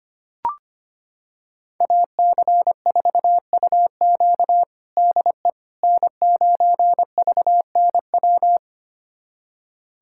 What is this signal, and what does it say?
Morse code, heard as a keyed tone, about 25 wpm, 700 Hz. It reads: AC4UQ DE N9VNW
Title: USA Callsign DE Callsign - 5 Characters - 25wpm